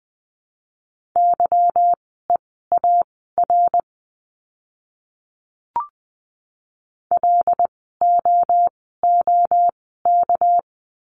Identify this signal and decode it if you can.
Morse code, heard as a keyed tone, about 20 wpm, spaced as for 14 wpm, 700 Hz. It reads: YEAR LOOK
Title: 244 Lesson 39f - Letter z - Words 20wpm